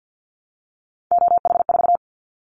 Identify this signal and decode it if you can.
Morse code, heard as a keyed tone, about 50 wpm, 700 Hz. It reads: OH4
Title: Callsign Prefixes for Top 50 DXCC - 50wpm